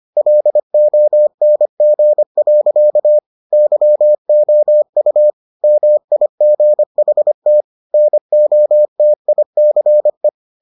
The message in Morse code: LONG. YOU MIGHT NOTICE

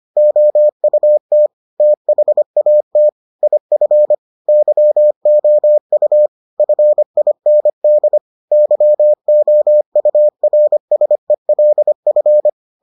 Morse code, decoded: OUT THAT IF YOU FIND YOURSELF

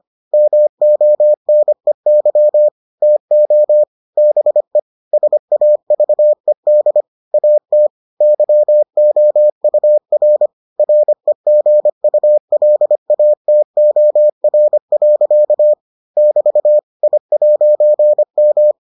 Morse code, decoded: MONEY TO BE SAVED AT YOUR REGULATOR. <BT> I'M

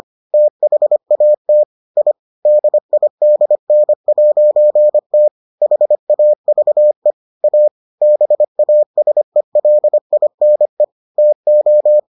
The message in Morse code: THAT I DIDN'T HAVE A BASELINE TO

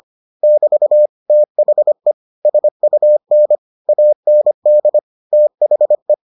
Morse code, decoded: <BT> THE SUN AND THE

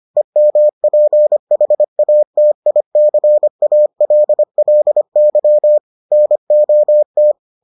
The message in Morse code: EMPHATICALLY NOT

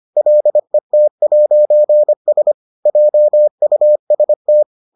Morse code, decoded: LET'S JUST

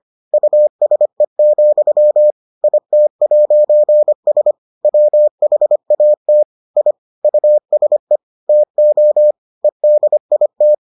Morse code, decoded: USE, IT'S WHAT I USE TO EDIT

